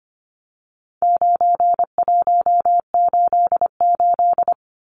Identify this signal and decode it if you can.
Morse code, heard as a keyed tone, about 25 wpm, 700 Hz. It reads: 9188
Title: Numbers - 4-Digit 25wpm